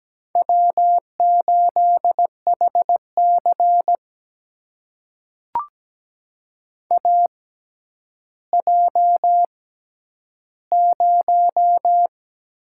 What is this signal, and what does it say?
Morse code, heard as a keyed tone, about 17 wpm, spaced as for 10 wpm, 700 Hz. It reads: W8HC AJ0
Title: USA Call Signs - 4 Characters - 6x Character Spacing - 17wpm